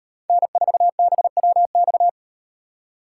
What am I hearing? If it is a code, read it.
Morse code, heard as a keyed tone, about 38 wpm, 700 Hz. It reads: N4BWX